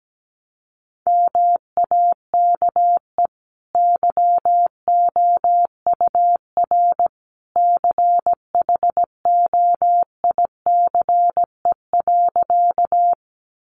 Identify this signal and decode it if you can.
Morse code, heard as a keyed tone, about 17 wpm, 700 Hz. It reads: MAKE YOUR CHOICE.
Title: Sentences from Top 900 Words 17wpm